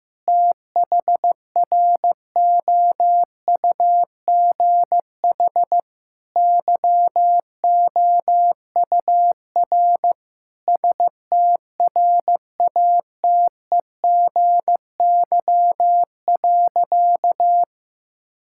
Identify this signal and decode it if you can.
Morse code, heard as a keyed tone, about 14 wpm, 700 Hz. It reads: THROUGH YOUR STRATEGY.
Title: Sentences from Top 1000 Words - Review Part 5 - 15wpm